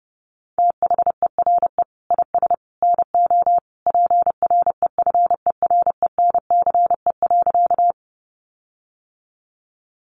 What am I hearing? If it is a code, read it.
Morse code, heard as a keyed tone, about 30 wpm, 700 Hz. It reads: THERE IS NO PREFERENCE.
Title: Sentences from Top 1500 Words 30wpm